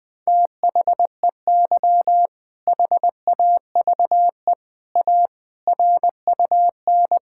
Morse code, heard as a keyed tone, about 20 wpm, 700 Hz. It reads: THEY HAVE A RUN
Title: Sentences from Top 1000 Words - Review Part 5 - 20wpm